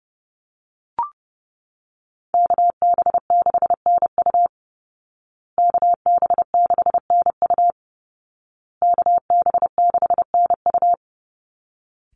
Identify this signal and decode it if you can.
Morse code, heard as a keyed tone, about 30 wpm, 700 Hz. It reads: KB6NU KB6NU KB6NU
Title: Call Signs Repeated 2x - 30wpm